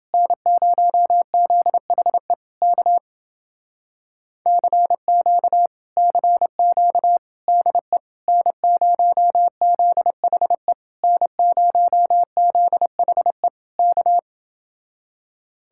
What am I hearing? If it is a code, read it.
Morse code, heard as a keyed tone, about 29 wpm, 700 Hz. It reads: N0ZHE K CQ CQ DE N0ZHE N0ZHE K